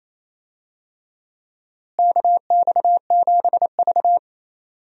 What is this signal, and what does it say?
Morse code, heard as a keyed tone, about 28 wpm, 700 Hz. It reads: KX7V
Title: USA Call Signs - 4 Characters - Repeated 3x - 28wpm